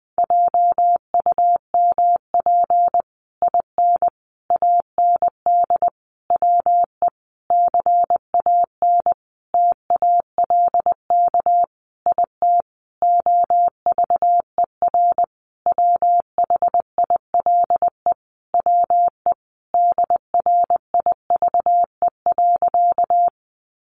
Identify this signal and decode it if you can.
Morse code, heard as a keyed tone, about 20 wpm, 700 Hz. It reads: JUMP IN AND WE CAN TALK IT OVER WHILE WE DRIVE.
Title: Sentences from Top 1300 Words 20wpm